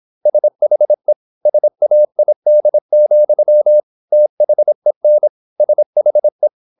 Morse code, heard as a keyed tone, about 26 wpm, 600 Hz. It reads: SHE SAID, THEN SHE
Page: Never 26wpm | CW med Gnister